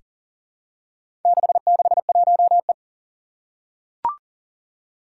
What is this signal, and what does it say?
Morse code, heard as a keyed tone, about 40 wpm, 700 Hz. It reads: 661E
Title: Four Challenging Letter-Number Combinations 40wpm